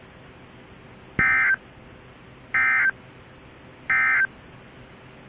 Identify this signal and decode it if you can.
Emergency Alert System (SAME) tones, heard as harsh buzzing bursts. It is heard from 1.1 to 4.3 s.
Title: Stillwater NOAA Weather radio